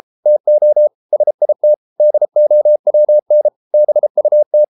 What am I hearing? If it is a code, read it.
Morse code, heard as a keyed tone, about 33 wpm, 600 Hz. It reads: TO SIT DOWN BUT